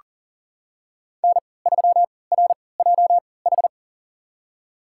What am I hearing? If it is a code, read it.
Morse code, heard as a keyed tone, about 37 wpm, spaced as for 22 wpm, 700 Hz. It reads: N3RJH
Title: USA Call Signs - 5 Characters - 3x Character Spacing - 40wpm